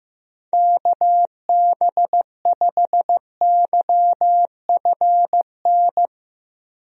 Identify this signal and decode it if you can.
Morse code, heard as a keyed tone, about 15 wpm, 700 Hz. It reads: KB5YFN
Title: USA Call Signs - 6 Characters - 15wpm